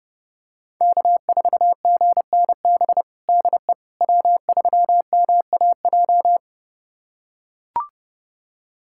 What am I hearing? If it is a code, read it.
Morse code, heard as a keyed tone, about 30 wpm, 700 Hz. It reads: K4GNB DE W3MAJ
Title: USA Callsign DE Callsign - 5 Characters - 30wpm